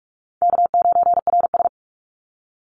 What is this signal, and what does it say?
Morse code, heard as a keyed tone, about 44 wpm, 700 Hz. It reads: K9RS
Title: Easy Call Signs - 45wpm